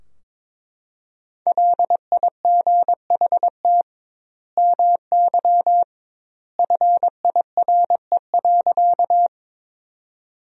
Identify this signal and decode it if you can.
Morse code, heard as a keyed tone, about 22 wpm, 700 Hz. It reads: LIGHT MY FIRE.
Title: 2x Word-Spacing - Sentences from Top 400 Words - 22wpm